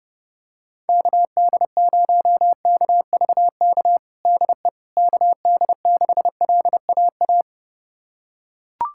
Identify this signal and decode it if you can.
Morse code, heard as a keyed tone, about 32 wpm, 700 Hz. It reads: KD0KVK DE KD6LAA
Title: USA Callsign DE Callsign - 6 Characters - 30wpm